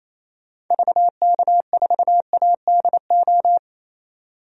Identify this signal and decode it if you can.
Morse code, heard as a keyed tone, about 28 wpm, 700 Hz. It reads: VK4ADO
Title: DXCC-100 Call Signs - 6 Characters - 28wpm